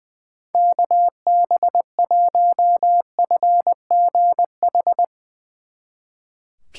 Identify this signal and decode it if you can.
Morse code, heard as a keyed tone, about 22 wpm, 700 Hz. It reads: KB1FGH
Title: Speed Racing - USA 2x3 Call Signs 20wpm